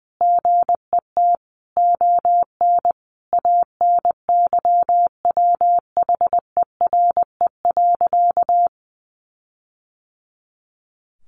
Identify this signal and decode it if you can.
Morse code, heard as a keyed tone, about 20 wpm, 700 Hz. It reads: GET ON ANYWHERE.
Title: Sentences from Top 1400 Words 20wpm